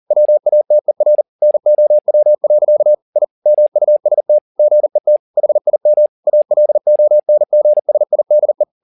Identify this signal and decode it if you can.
Morse code, heard as a keyed tone, about 40 wpm, 600 Hz. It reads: WATER NOW. I MUST GET HIM ALONGSIDE
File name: the old man and the sea 2 40wpm